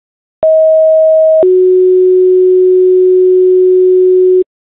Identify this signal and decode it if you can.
Two-tone page, heard as a one-second tone, then a three-second tone, about 640 then 370 Hz, with noise. It is heard from 0.4 to 4.4 s.